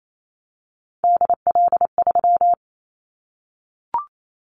Morse code, heard as a keyed tone, about 28 wpm, 700 Hz. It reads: DL3
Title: Callsign Prefixes for Top 25 DXCC - 28wpm